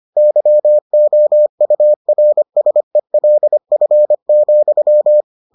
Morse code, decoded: YOURSELF,

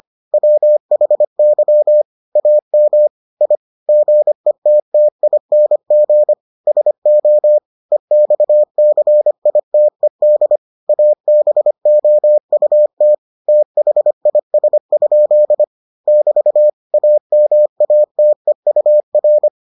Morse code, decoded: WHY AM I GETTING SO EXCITED ABOUT THIS? <BT> AMATEUR

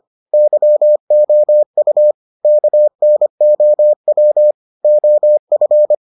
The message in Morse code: YOU KNOW OF

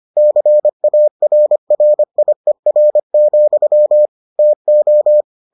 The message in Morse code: CARRIER, TO

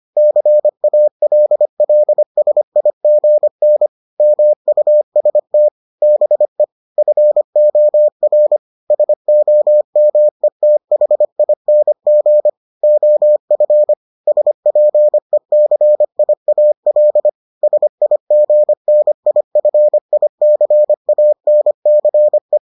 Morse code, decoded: CALLSIGN MUST BE FOR SOMETHING OF SPECIAL SIGNIFICANCE

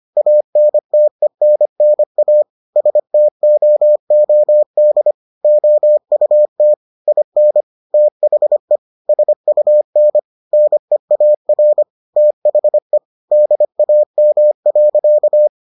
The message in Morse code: ANTENNA STOOD OUT IN THE SUN NEAR THE DAM.

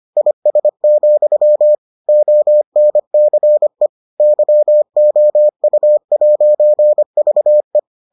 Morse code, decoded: IS, ONCE YOU'VE

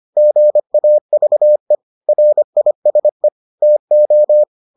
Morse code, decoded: GAVE RISE TO

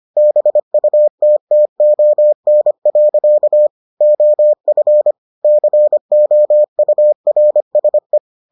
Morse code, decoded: BUTTON. OF COURSE

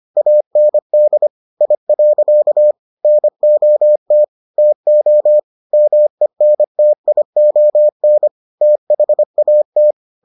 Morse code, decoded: AND I. NOT TO MENTION THAT

